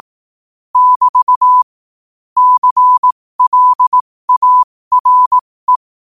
Morse code, heard as a keyed tone, about 18 wpm, spaced as for 15 wpm, 1000 Hz. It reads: <BT> CLARE
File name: 15wpm